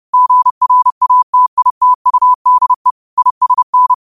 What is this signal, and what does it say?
Morse code, heard as a keyed tone, about 30 wpm, 1000 Hz. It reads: GRATITUDE ISN
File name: qotdfiles / 30wpm